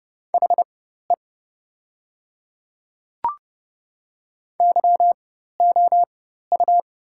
Morse code, heard as a keyed tone, about 30 wpm, spaced as for 14 wpm, 700 Hz. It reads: HE YOU
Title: 4x Character Spacing - Top 200 Words - 30wpm